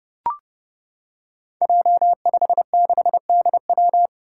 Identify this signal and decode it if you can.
Morse code, heard as a keyed tone, about 30 wpm, 700 Hz. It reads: J56DW
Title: Five Challenging Letter-Number Combinations 30wpm